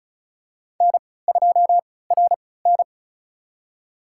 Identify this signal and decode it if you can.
Morse code, heard as a keyed tone, about 34 wpm, spaced as for 20 wpm, 700 Hz. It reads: N2RN
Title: USA Call Signs - 4 Characters - 3x Character Spacing - 35wpm